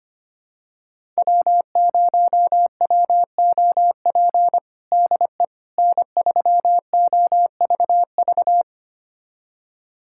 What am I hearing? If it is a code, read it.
Morse code, heard as a keyed tone, about 25 wpm, 700 Hz. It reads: W0WOP DE N3OVV
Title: USA Callsign DE Callsign - 5 Characters - 25wpm